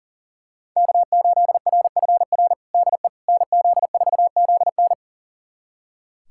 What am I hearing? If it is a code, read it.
Morse code, heard as a keyed tone, about 37 wpm, 700 Hz. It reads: K8RFR DE NZ4ZN
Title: USA Callsign DE Callsign - 5 Characters - 40wpm